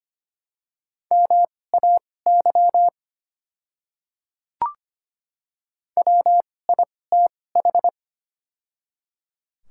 Morse code, heard as a keyed tone, about 25 wpm, spaced as for 18 wpm, 700 Hz. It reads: MAY WITH
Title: 2x Character Spacing - Top 500 Words - 25wpm